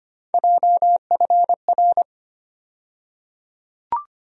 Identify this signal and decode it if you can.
Morse code, heard as a keyed tone, about 25 wpm, 700 Hz. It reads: JFR
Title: USA Call Sign Postfixes - 3 Letters 25wpm